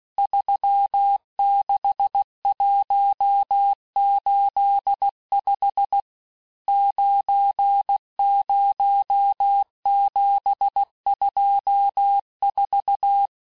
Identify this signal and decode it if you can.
Morse code, heard as a keyed tone, about 16 wpm, 800 Hz. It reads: 36185 90724
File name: cours-du-lundi files / Cours-1586-texte-chiffre-vitesse-15mpm